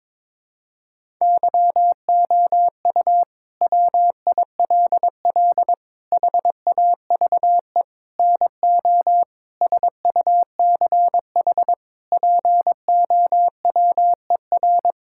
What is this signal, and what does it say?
Morse code, heard as a keyed tone, about 22 wpm, 700 Hz. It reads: YOU WILL HAVE NO SUCH POWER
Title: Sentences from Top 1000 Words - Review Part 6 - 22wpm